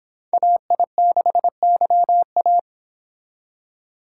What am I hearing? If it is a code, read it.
Morse code, heard as a keyed tone, about 31 wpm, 700 Hz. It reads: AI6YA